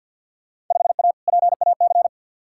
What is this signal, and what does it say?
Morse code, heard as a keyed tone, about 52 wpm, 700 Hz. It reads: 5A PAC
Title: Field Day Exchanges Encore - 50wpm